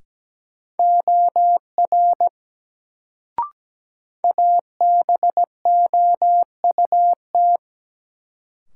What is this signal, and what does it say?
Morse code, heard as a keyed tone, about 17 wpm, 700 Hz. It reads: OR ABOUT
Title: Sentences from Top 100 Words - New Word Drill - 17wpm